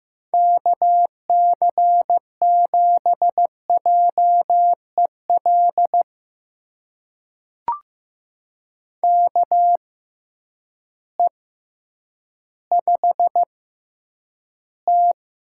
Morse code, heard as a keyed tone, about 15 wpm, spaced as for 9 wpm, 700 Hz. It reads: KC7JEL KE5T
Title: USA Call Signs - 6 Characters - 6x Character Spacing - 15wpm